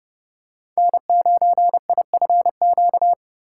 Morse code, heard as a keyed tone, about 30 wpm, 700 Hz. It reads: N9IFQ
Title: USA Call Signs - 5 Characters - 8x Character Spacing - 30wpm